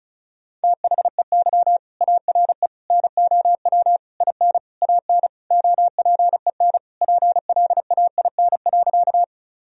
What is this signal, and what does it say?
Morse code, heard as a keyed tone, about 34 wpm, 700 Hz. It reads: THEY ARE NOW IN AN OPEN PLAIN.
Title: Sentences from Top 1000 Words - Review Part 2 - 35wpm